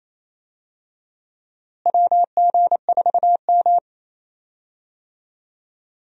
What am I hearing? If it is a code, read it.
Morse code, heard as a keyed tone, about 28 wpm, 700 Hz. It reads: WG4M
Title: USA Call Signs - 4 Characters - Repeated 2x - 28wpm